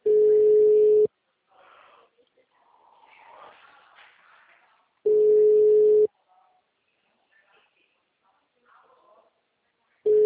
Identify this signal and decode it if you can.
Ringback tone (Europe, 425 Hz), with speech.